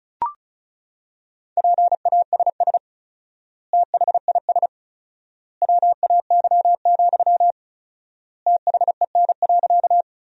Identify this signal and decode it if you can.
Morse code, heard as a keyed tone, about 34 wpm, 700 Hz. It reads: PASS THIS WAY, THEN.
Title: 4x Word-Spacing - Sentences from Top 400 Words - 35wpm